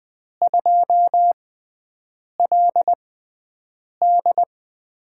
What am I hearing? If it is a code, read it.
Morse code, heard as a keyed tone, about 20 wpm, spaced as for 7 wpm, 700 Hz. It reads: 2LD
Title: USA Call Signs - 6x Character Spacing - 20wpm